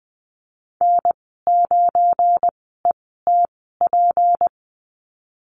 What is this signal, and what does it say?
Morse code, heard as a keyed tone, about 20 wpm, spaced as for 15 wpm, 700 Hz. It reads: N9ETP
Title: 173 Lesson 28g - Number 9 - Call Signs 20wpm